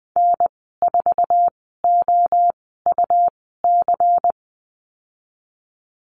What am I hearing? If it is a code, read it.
Morse code, heard as a keyed tone, about 20 wpm, spaced as for 14 wpm, 700 Hz. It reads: N4OUC
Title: 128 Lesson 21g - Number 6 - Call Signs 20wpm